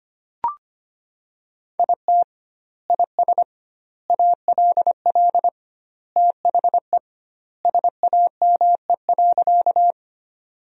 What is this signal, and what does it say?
Morse code, heard as a keyed tone, about 25 wpm, 700 Hz. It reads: IT IS ALL THE SAME.
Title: 2x Word-Spacing - Sentences from Top 200 Words - 25wpm